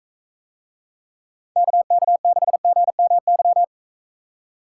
Morse code, heard as a keyed tone, about 43 wpm, 700 Hz. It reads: KK6GMY